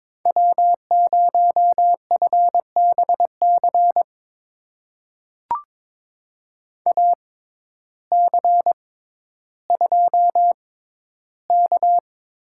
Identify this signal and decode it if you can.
Morse code, heard as a keyed tone, about 23 wpm, spaced as for 12 wpm, 700 Hz. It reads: W0FBC AC2K
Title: USA Call Signs - 5 Characters - 6x Character Spacing - 22wpm